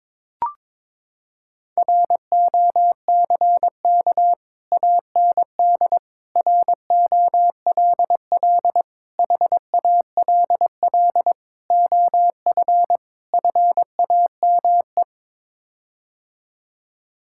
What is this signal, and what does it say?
Morse code, heard as a keyed tone, about 22 wpm, 700 Hz. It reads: ROCK AND ROLL HALL OF FAME
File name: practice / 22wpm / Sets of 6 Words - 22wpm